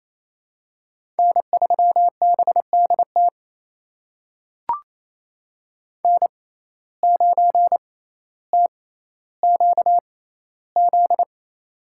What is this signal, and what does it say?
Morse code, heard as a keyed tone, about 28 wpm, spaced as for 14 wpm, 700 Hz. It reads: N3BDT N9TQZ
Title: USA Call Signs - 5 Characters - 6x Character Spacing - 28wpm